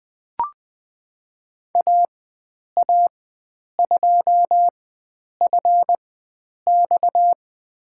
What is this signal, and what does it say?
Morse code, heard as a keyed tone, about 20 wpm, spaced as for 9 wpm, 700 Hz. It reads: AA2FX
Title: USA Call Signs - 5 Characters - 4x Character Spacing - 20wpm